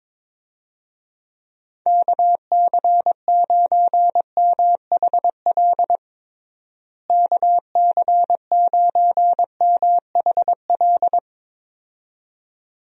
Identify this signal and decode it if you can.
Morse code, heard as a keyed tone, about 22 wpm, 700 Hz. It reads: KC9MHL KC9MHL
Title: USA Call Signs - 6 Characters - Repeated 2x - 22wpm